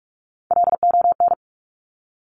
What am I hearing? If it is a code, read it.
Morse code, heard as a keyed tone, about 46 wpm, 700 Hz. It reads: LON
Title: Top Three Letter Combinations in Words 45wpm